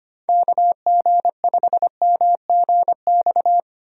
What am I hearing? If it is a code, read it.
Morse code, heard as a keyed tone, about 25 wpm, 700 Hz. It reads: KG5MGX
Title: USA Call Signs - 6 Characters - 25wpm